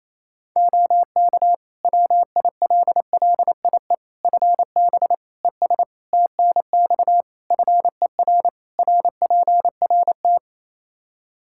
Morse code, heard as a keyed tone, about 27 wpm, 700 Hz. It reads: OK WILLIE FB ES TNX FER RPRT
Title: Practice QSO - Exchange 2 - Info - 28wpm